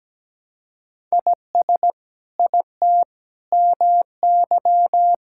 Morse code, heard as a keyed tone, about 17 wpm, 700 Hz. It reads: IS IT MY
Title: Sentences from Top 1000 Words - Review Part 5 - 17wpm